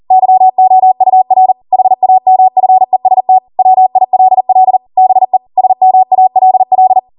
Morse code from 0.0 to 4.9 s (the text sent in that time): YOUR HAMFEST WILL